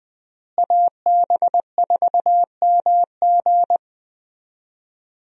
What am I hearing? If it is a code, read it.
Morse code, heard as a keyed tone, about 20 wpm, 700 Hz. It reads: AB4MG
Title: USA Call Signs - 5 Characters - Repeated 2x - 20wpm